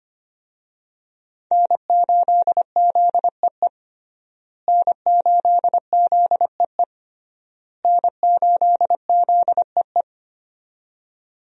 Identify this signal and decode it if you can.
Morse code, heard as a keyed tone, about 25 wpm, 700 Hz. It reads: N8ZEE N8ZEE N8ZEE
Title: USA Call Signs - 5 Characters - Repeated 3x - 25wpm